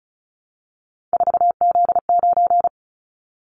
Morse code, heard as a keyed tone, about 36 wpm, 690 Hz. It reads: 4Z9
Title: Callsign Prefixes for Top 50 DXCC - 35wpm